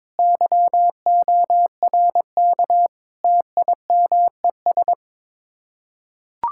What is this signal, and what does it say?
Morse code, heard as a keyed tone, about 22 wpm, 700 Hz. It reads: YORK TIMES
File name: practice / 22wpm / Build A Phrase - 5-Word Phrases - 22wpm